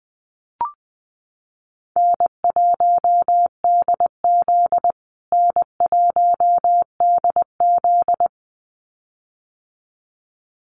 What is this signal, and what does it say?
Morse code, heard as a keyed tone, about 20 wpm, 700 Hz. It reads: N1DZ N1DZ
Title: USA Call Signs Repeated 2x 20wpm